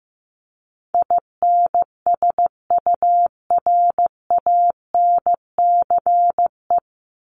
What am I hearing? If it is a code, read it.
Morse code, heard as a keyed tone, about 15 wpm, 700 Hz. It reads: INSURANCE
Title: Top 1001-1250 Words 15wpm